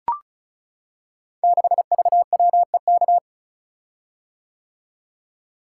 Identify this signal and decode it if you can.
Morse code, heard as a keyed tone, about 36 wpm, 700 Hz. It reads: 6VWEK
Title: Code Groups - 5 Letter:Numbers - 35wpm